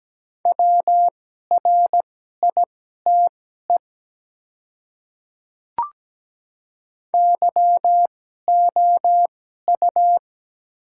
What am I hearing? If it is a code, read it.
Morse code, heard as a keyed tone, about 17 wpm, spaced as for 12 wpm, 700 Hz. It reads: WRITE YOU
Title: 2x Character Spacing - Top 100 Words - 17wpm